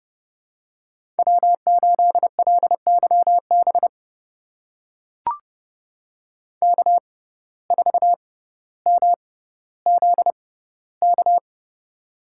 Morse code, heard as a keyed tone, about 30 wpm, spaced as for 15 wpm, 700 Hz. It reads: W8LYB K4MZK
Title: USA Call Signs - 5 Characters - 6x Character Spacing - 30wpm